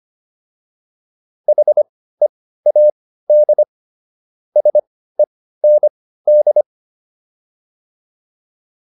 Morse code, heard as a keyed tone, about 25 wpm, spaced as for 15 wpm, 600 Hz. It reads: HEAD SEND